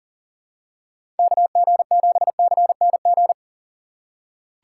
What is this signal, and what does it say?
Morse code, heard as a keyed tone, about 40 wpm, 700 Hz. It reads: KG7CNG
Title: USA Call Signs - 6 Characters - 3x Character Spacing - 40wpm